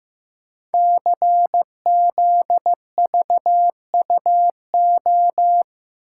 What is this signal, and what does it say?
Morse code, heard as a keyed tone, about 15 wpm, 700 Hz. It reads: CZVUO
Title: Code Groups - 5 Letter:Numbers - 15wpm